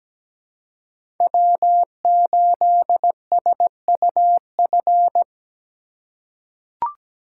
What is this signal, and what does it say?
Morse code, heard as a keyed tone, about 17 wpm, 700 Hz. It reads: W8SUF
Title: USA Call Signs - 5 Characters - 2x Character Spacing - 17wpm